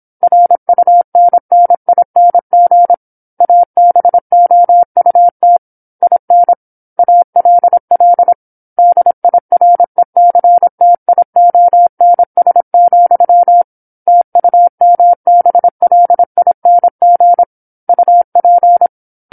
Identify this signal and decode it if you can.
Morse code, heard as a keyed tone, about 27 wpm, 700 Hz. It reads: RUNNING ABOUT IN ALL DIRECTIONS, TUMBLING UP
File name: 08-Chapter-08 26 WPM